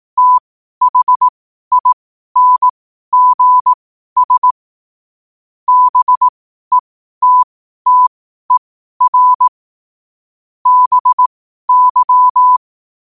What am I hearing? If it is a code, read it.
Morse code, heard as a keyed tone, about 18 wpm, spaced as for 12 wpm, 1000 Hz. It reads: THINGS BETTER BY
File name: qotdfiles / 12wpm